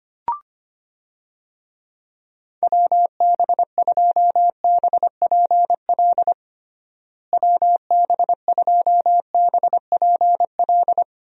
Morse code, heard as a keyed tone, about 25 wpm, 700 Hz. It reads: WB2BPL WB2BPL
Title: USA Call Signs - 6 Characters - Repeated 3x - 25wpm